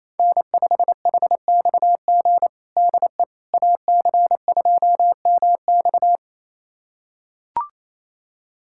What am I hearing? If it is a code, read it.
Morse code, heard as a keyed tone, about 28 wpm, 700 Hz. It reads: N5HXG DE AC2MX
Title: USA Callsign DE Callsign - 5 Characters - 28wpm